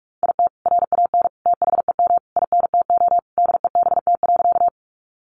Morse code, heard as a keyed tone, about 46 wpm, 700 Hz. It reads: IT RAN THEM INTO DEBT.
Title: Sentences from Top 1300 Words 45wpm